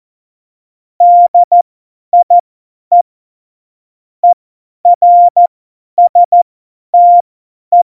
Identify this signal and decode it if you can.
Morse code, heard as a keyed tone, about 14 wpm, spaced as for 10 wpm, 700 Hz. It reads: DIE ERSTE
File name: Morsekurs / Witze / Witz 08. Januar 2019, 50 F70